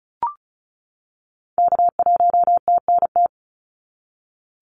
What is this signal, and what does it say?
Morse code, heard as a keyed tone, about 34 wpm, 700 Hz. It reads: K1TNT
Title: NA SOTA Activator Callsigns 35wpm